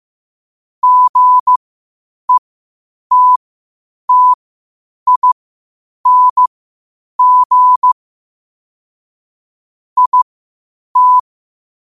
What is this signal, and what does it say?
Morse code, heard as a keyed tone, about 15 wpm, spaced as for 8 wpm, 1000 Hz. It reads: GETTING IT
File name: qotdfiles / 8wpm